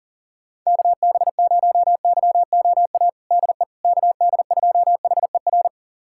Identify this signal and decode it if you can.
Morse code, heard as a keyed tone, about 40 wpm, 700 Hz. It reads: KB0YOA DE KD2HER
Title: USA Callsign DE Callsign - 6 Characters - 40wpm